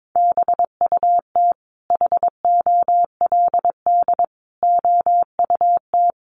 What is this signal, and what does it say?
Morse code, heard as a keyed tone, about 22 wpm, 700 Hz. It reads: BUT HOLD OUT
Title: Sentences from Top 1300 Words 22wpm